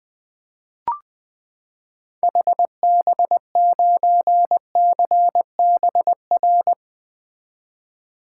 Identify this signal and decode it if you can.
Morse code, heard as a keyed tone, about 21 wpm, 700 Hz. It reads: HB9CBR
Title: 2023 SOTA Activator Call Signs - 20wpm